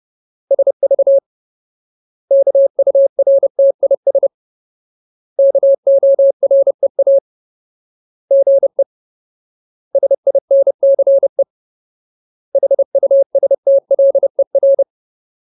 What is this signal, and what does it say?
Morse code, heard as a keyed tone, about 30 wpm, 550 Hz. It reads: SV KURTIS KOREA GE SINCE HUSTLER
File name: Ultimate Word List / 30WPM Ultimate